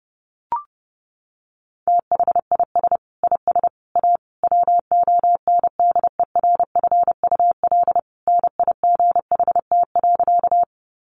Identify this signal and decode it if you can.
Morse code, heard as a keyed tone, about 30 wpm, 700 Hz. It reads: THIS IS A WONDERFUL NIGHT.
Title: Sentences from Top 1100 Words 30wpm